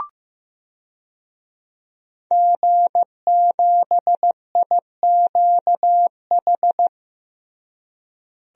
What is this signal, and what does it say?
Morse code, heard as a keyed tone, about 15 wpm, 700 Hz. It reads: G7IQH
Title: DXCC-100 Call Signs - 5 Characters - Repeated 2x - 15wpm